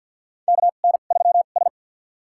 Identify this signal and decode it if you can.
Morse code, heard as a keyed tone, about 50 wpm, spaced as for 36 wpm, 700 Hz. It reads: KN3S
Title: USA Call Signs - 4 Characters - 2x Character Spacing - 50wpm